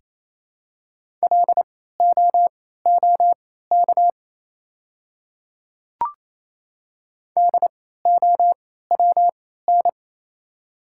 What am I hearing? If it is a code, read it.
Morse code, heard as a keyed tone, about 28 wpm, spaced as for 16 wpm, 700 Hz. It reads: LOOK DOWN
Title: 3x Character Spacing - Top 100 Words - 28wpm